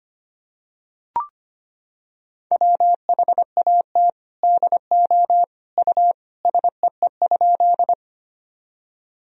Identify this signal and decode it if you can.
Morse code, heard as a keyed tone, about 25 wpm, 700 Hz. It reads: WHAT DO U SEE?
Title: Sentences from Top 100 Words and Abbreviations 25wpm